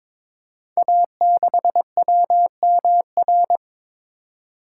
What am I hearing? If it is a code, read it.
Morse code, heard as a keyed tone, about 22 wpm, 700 Hz. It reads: A6WMR
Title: Code Groups - 5 Letter:Numbers - 22wpm